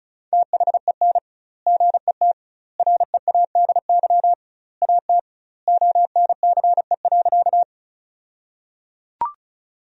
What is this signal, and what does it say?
Morse code, heard as a keyed tone, about 34 wpm, 700 Hz. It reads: THEN GET READY AT ONCE.
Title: 2x Word-Spacing - Sentences from Top 400 Words - 35wpm